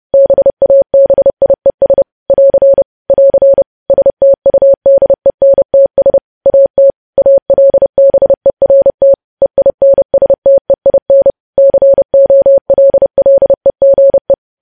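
Morse code, heard as a keyed tone, about 30 wpm, 550 Hz. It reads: BABIES <AR> <AR> STUDENTS AT ALBERT EINSTEIN COLLEGE
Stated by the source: News headlines sent in Morse code at 30.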